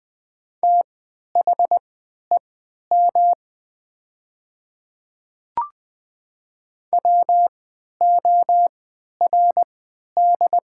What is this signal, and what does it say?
Morse code, heard as a keyed tone, about 20 wpm, spaced as for 11 wpm, 700 Hz. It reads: THEM WORD
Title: 3x Character Spacing - Top 100 Words - 20wpm